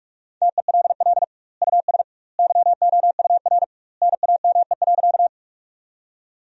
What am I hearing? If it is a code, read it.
Morse code, heard as a keyed tone, about 45 wpm, 700 Hz. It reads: TELL US YOUR NAME.
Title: 2x Word-Spacing - Sentences from Top 200 Words - 45wpm